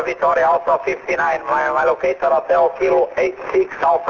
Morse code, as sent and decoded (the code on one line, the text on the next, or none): none